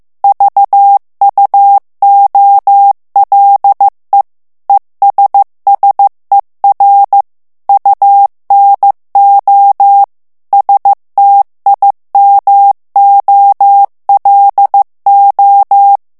Morse code, heard as a keyed tone, about 15 wpm, 800 Hz. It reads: VUOLE ESSER UNO STIMOLO
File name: ITALIEN texte clair 1 vitesse 14 mots minute